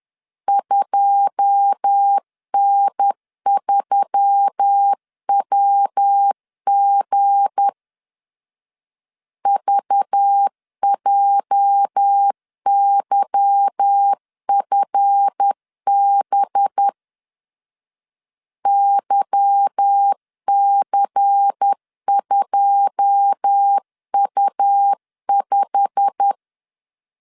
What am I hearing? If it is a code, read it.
Morse code, heard as a keyed tone, about 10 wpm, 800 Hz. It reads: 2N3WG VJYFB YC2U5
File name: LEZIONI / 9 2 10wpm 2736